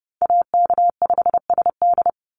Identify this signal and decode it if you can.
Morse code, heard as a keyed tone, about 30 wpm, 700 Hz. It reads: AK5SD
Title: NA SOTA Chaser Callsigns 30wpm